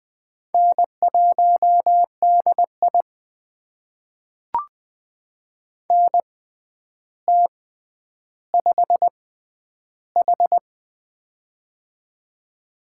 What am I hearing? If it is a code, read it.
Morse code, heard as a keyed tone, about 20 wpm, spaced as for 10 wpm, 700 Hz. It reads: N1DI NT5H
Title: USA Call Signs - 4 Characters - 6x Character Spacing - 20wpm